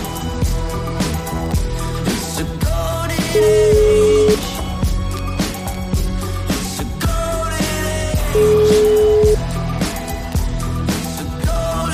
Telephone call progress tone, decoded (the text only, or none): ringback tone